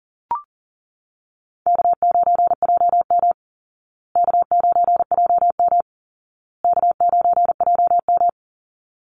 Morse code, heard as a keyed tone, about 40 wpm, 700 Hz. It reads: K9JM K9JM K9JM
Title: Easy Call Signs Repeated 3x - 40wpm